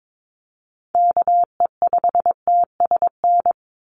Morse code, heard as a keyed tone, about 22 wpm, 700 Hz. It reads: KE5TSN
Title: USA Call Signs 22wpm